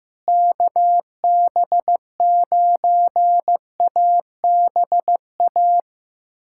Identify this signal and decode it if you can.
Morse code, heard as a keyed tone, about 15 wpm, 700 Hz. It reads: KB9ABA
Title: USA Callsign DE Callsign - 6 Characters - 15wpm